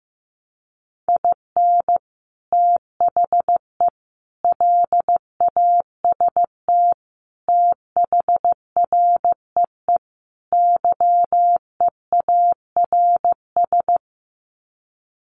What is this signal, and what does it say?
Morse code, heard as a keyed tone, about 15 wpm, 700 Hz. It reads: IN THE LAST THREE YEARS
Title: Sets of 5 Words Encore2 15wpm